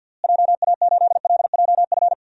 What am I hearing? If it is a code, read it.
Morse code, heard as a keyed tone, about 47 wpm, 700 Hz. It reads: JA8LJF
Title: DXCC-100 Call Signs - 6 Characters - Repeated 3x - 50wpm